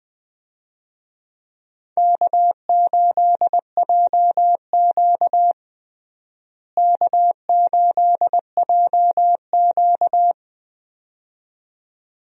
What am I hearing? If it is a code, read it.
Morse code, heard as a keyed tone, about 20 wpm, 700 Hz. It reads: K8JQ K8JQ
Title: USA Call Signs - 4 Characters - Repeated 2x - 20wpm